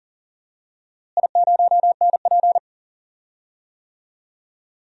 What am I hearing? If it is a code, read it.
Morse code, heard as a keyed tone, about 40 wpm, 700 Hz. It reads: I0NP
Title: DXCC-100 Call Signs - 4 Characters - Repeated 3x - 40wpm